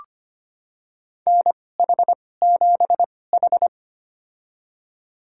Morse code, heard as a keyed tone, about 25 wpm, spaced as for 18 wpm, 700 Hz. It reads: NH7H
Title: USA Call Signs - 2x Character Spacing - 25wpm